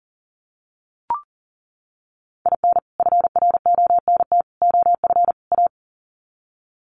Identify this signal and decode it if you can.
Morse code, heard as a keyed tone, about 40 wpm, 700 Hz. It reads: IN FRONT OF A
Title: Sets of 4 Words 40wpm